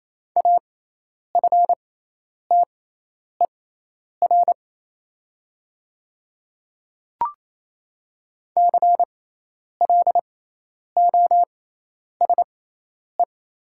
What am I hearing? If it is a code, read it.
Morse code, heard as a keyed tone, about 28 wpm, spaced as for 10 wpm, 700 Hz. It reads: AFTER CLOSE
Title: 6x Character Spacing - Top 500 Words - 28wpm